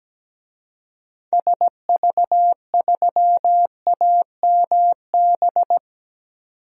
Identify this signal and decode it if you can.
Morse code, heard as a keyed tone, about 17 wpm, 700 Hz. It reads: SV3AMB
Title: DXCC-100 Call Signs - 6 Characters - 17wpm